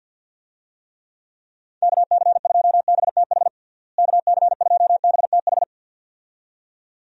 Morse code, heard as a keyed tone, about 48 wpm, 700 Hz. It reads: KC2BTH KC2BTH
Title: USA Call Signs - 6 Characters - Repeated 2x - 50wpm